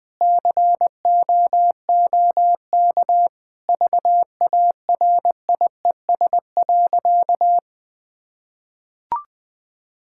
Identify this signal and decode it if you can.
Morse code, heard as a keyed tone, about 20 wpm, 700 Hz. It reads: COOK VARIES.
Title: Sentences with 7-Word Phrases - 20wpm